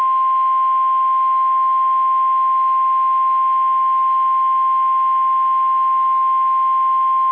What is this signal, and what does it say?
Emergency Alert System (SAME) tones (the 1050 Hz attention tone). It is heard from 0.3 to 7.0 s.
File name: eas / KSDZKDJL / KSDZKDJL